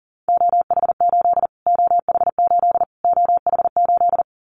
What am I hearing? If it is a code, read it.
Morse code, heard as a keyed tone, about 40 wpm, 700 Hz. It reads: OH8 OH8 OH8
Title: Callsign Prefixes for Top 25 DXCC - Repeated 3x - 40wpm